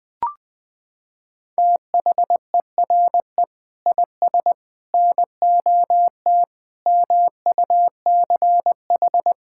Morse code, heard as a keyed tone, about 20 wpm, 700 Hz. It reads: THERE IS NOT MUCH
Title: Sentences from Top 1600 Words 20wpm